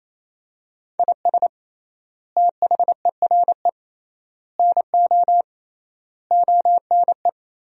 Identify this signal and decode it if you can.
Morse code, heard as a keyed tone, about 27 wpm, 700 Hz. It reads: IS THERE NO ONE
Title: 3x Word-Spacing - Sentences from Top 200 Words - 28wpm